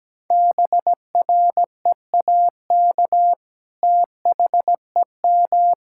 Morse code, heard as a keyed tone, about 17 wpm, 700 Hz. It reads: BREAK THEM
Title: Sentences from Top 1000 Words - Review Part 6 - 17wpm